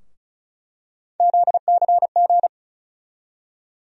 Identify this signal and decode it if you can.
Morse code, heard as a keyed tone, about 36 wpm, 700 Hz. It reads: ZCG
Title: USA Call Sign Postfixes - 3 Letters 35wpm